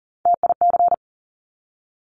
Morse code, heard as a keyed tone, about 40 wpm, 700 Hz. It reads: TIC
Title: Top Three Letter Combinations in Words 40wpm